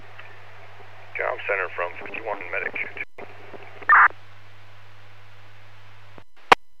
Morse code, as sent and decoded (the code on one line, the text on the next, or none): none